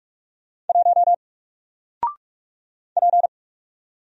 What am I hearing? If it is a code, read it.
Morse code, heard as a keyed tone, about 44 wpm, 700 Hz. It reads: 1 P
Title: Single Letter-Number Alt 45wpm